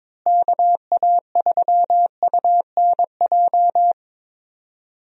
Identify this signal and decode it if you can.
Morse code, heard as a keyed tone, about 22 wpm, 700 Hz. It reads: KA3UNJ
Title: USA Call Signs - 6 Characters - Repeated 3x - 22wpm